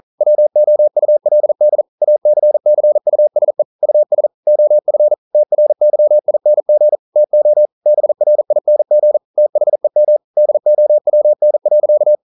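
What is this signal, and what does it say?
Morse code, heard as a keyed tone, about 42 wpm, 600 Hz. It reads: WOULD ACCUSE US OF TRYING TO BRING THEM DOWN.
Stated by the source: Morse code practice at 41 wpm, from CW med Gnister.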